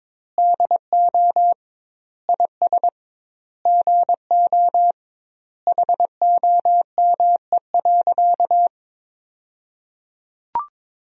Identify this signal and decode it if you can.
Morse code, heard as a keyed tone, about 22 wpm, 700 Hz. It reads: DO IS GO HOME.
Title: 2x Word-Spacing - Sentences from Top 200 Words - 22wpm